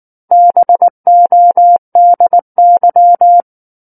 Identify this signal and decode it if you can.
Morse code, heard as a keyed tone, about 19 wpm, 700 Hz. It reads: BODY
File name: Book One / 10-Chapter 10 - 19 WPM